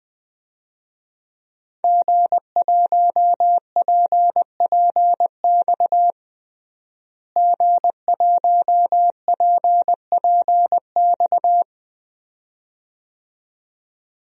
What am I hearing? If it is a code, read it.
Morse code, heard as a keyed tone, about 20 wpm, 700 Hz. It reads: G1PPX G1PPX
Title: DXCC-100 Call Signs - 5 Characters - Repeated 2x - 20wpm